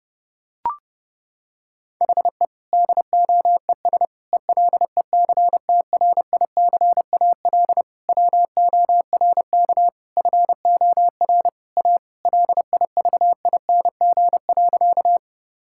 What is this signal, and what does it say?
Morse code, heard as a keyed tone, about 30 wpm, 700 Hz. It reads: HE DOES ELECTRICAL WORK FOR A LIVING.
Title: Sentences from Top 1000 Words - Review Part 6 - 30wpm